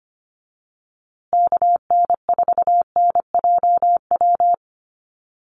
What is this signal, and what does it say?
Morse code, heard as a keyed tone, about 25 wpm, 700 Hz. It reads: KN4NJW
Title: USA Call Signs Repeated 2x 25wpm